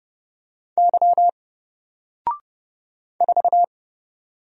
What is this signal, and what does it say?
Morse code, heard as a keyed tone, about 30 wpm, 700 Hz. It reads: Y 4
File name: Single-Letter-Number-30wpm